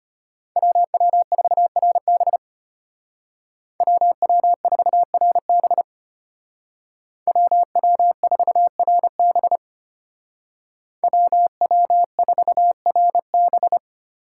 Morse code, heard as a keyed tone, about 33 wpm, 700 Hz. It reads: WW4RB WW4RB WW4RB WW4RB